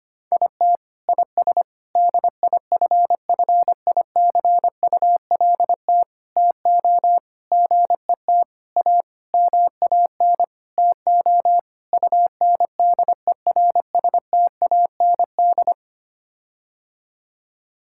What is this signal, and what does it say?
Morse code, heard as a keyed tone, about 25 wpm, 700 Hz. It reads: IT IS DIFFICULT TO GET A MAN TO UNDERSTAND
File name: Sets of 9 Words - 25wpm